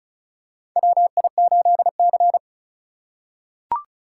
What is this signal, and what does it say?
Morse code, heard as a keyed tone, about 34 wpm, 700 Hz. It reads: WI8C
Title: USA Call Signs - 4 Characters - 2x Character Spacing - 35wpm